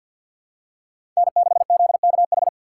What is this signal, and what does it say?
Morse code, heard as a keyed tone, about 52 wpm, 700 Hz. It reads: N6ZKH
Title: USA Call Signs - 5 Characters - 6x Character Spacing - 50wpm